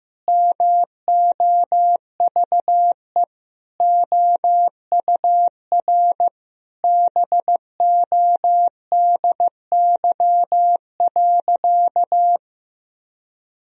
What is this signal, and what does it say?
Morse code, heard as a keyed tone, about 16 wpm, 700 Hz. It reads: MOVE OUR BODY.
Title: Sentences from Top 1000 Words - Review Part 2 - 15wpm